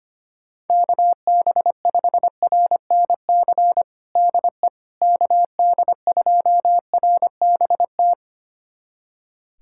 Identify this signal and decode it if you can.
Morse code, heard as a keyed tone, about 25 wpm, 700 Hz. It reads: KB5RNC DE KD2RBT
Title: USA Callsign DE Callsign - 6 Characters - 25wpm